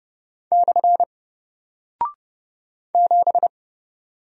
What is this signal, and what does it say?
Morse code, heard as a keyed tone, about 30 wpm, 700 Hz. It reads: / 7
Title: Single Character (Letter:Number:Punctuation:Prosign) 30wpm